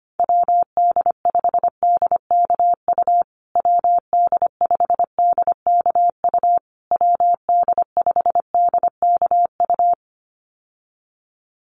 Morse code, heard as a keyed tone, about 25 wpm, 700 Hz. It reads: WD5DKU WD5DKU WD5DKU
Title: USA Call Signs Repeated 3x 25wpm